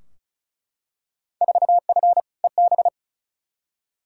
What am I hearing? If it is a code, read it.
Morse code, heard as a keyed tone, about 34 wpm, 700 Hz. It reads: 4F EB
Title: Field Day Exchanges - 35wpm